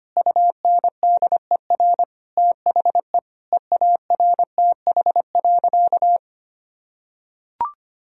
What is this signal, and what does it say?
Morse code, heard as a keyed tone, about 25 wpm, 700 Hz. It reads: UNDER THE EARTH.
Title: Sentences from Top 1000 Words - Review Part 1 - 25wpm